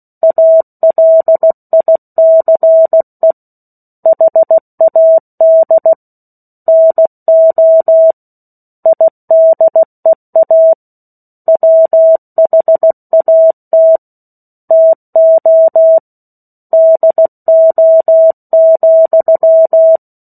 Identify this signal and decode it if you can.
Morse code, heard as a keyed tone, about 16 wpm, 650 Hz. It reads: ALICE HAD NO IDEA WHAT TO DO,